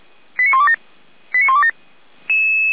Five-tone selective call (ZVEI), heard as quick runs of five tones, tones of about 70 ms. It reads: 89127 89127